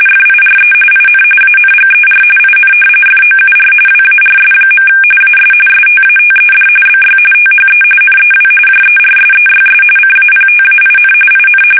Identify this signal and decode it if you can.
RTTY, about 45.45 baud. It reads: FIRE IN ANGER IN FOUR WARS. AS STATED BY THE NEW JERSEYS CAPTAIN IN
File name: fsk850 / battleshipnj